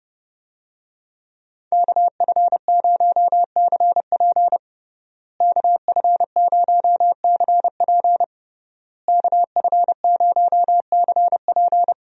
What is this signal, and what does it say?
Morse code, heard as a keyed tone, about 32 wpm, 700 Hz. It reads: KF0CP KF0CP KF0CP
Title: USA Call Signs - 5 Characters - Repeated 3x - 30wpm